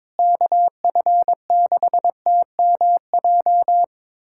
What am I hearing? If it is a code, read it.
Morse code, heard as a keyed tone, about 22 wpm, 700 Hz. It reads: KF6TMJ
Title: USA Callsign DE Callsign - 6 Characters - 22wpm